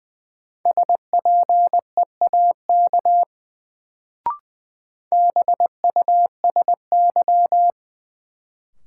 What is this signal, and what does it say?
Morse code, heard as a keyed tone, about 21 wpm, 700 Hz. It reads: SPEAK BUSY
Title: Sentences from Top 1000 Words - Review Part 2 - New Word Drill - 20wpm